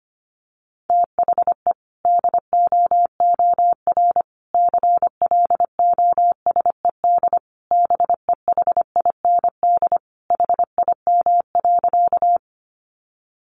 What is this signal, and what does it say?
Morse code, heard as a keyed tone, about 25 wpm, 700 Hz. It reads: THE DOOR CLOSED BEHIND HIM.
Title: Sentences from Top 1400 Words 25wpm